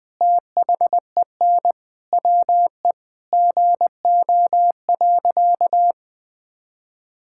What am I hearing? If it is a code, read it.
Morse code, heard as a keyed tone, about 20 wpm, 700 Hz. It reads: THEN WE GO.
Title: Sentences with 2-Word Phrases - Spoken Beginning - 20wpm